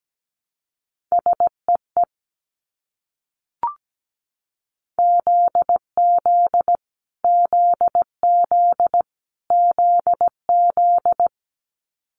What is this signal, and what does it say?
Morse code, heard as a keyed tone, about 17 wpm, 700 Hz. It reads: SEE ZZ ZZ ZZ
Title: Multi-letter Phonemes Repeated 3x 17wpm